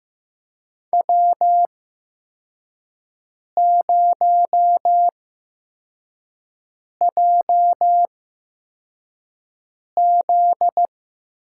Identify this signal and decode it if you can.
Morse code, heard as a keyed tone, about 15 wpm, spaced as for 4 wpm, 700 Hz. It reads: W0JZ
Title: USA Call Signs - 4 Characters - 8x Character Spacing - 15wpm